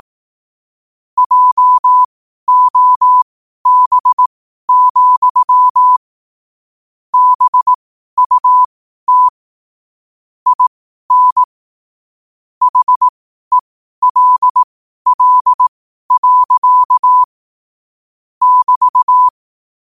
Morse code, heard as a keyed tone, about 18 wpm, spaced as for 12 wpm, 1000 Hz. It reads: JOB, BUT IN HELL. <BT>
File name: qotdfiles / 12wpm